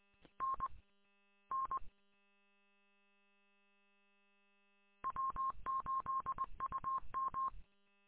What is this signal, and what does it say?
Morse code, heard as a keyed tone, about 20 wpm, 1070 Hz. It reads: N N W8UM
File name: w8um- -combined